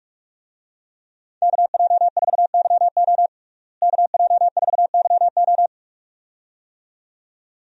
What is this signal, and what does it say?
Morse code, heard as a keyed tone, about 42 wpm, 700 Hz. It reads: KJ4YO KJ4YO
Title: USA Call Signs - 5 Characters - Repeated 2x - 45wpm